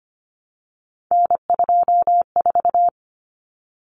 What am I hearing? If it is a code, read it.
Morse code, heard as a keyed tone, about 25 wpm, 700 Hz. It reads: N24
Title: Common Cut Numbers - 3-Digit 25wpm